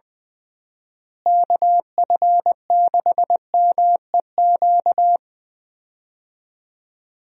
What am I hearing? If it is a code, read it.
Morse code, heard as a keyed tone, about 20 wpm, 700 Hz. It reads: KF6MEQ
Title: USA Call Signs - 6 Characters - Repeated 2x - 20wpm